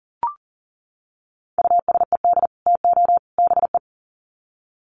Morse code, heard as a keyed tone, about 40 wpm, 700 Hz. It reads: USED TO BE
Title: Sets of 3 Words 40wpm